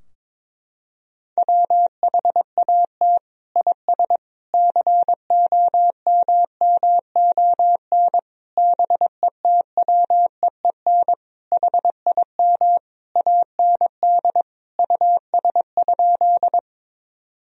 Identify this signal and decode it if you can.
Morse code, heard as a keyed tone, about 22 wpm, 700 Hz. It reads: WHAT IS COMMON BETWEEN HIM AND US?
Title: Sentences from Top 1000 Words - Review Part 3 - 22wpm